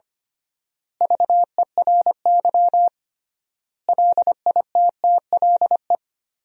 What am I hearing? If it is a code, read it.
Morse code, heard as a keyed tone, about 25 wpm, 700 Hz. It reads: VERY LITTLE
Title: 3x Word-Spacing - Sentences from Top 400 Words - 25wpm